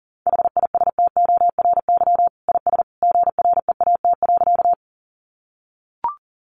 Morse code, heard as a keyed tone, about 40 wpm, 700 Hz. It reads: HISTORY IS GREAT.
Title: Sentences from Top 600 Words 40wpm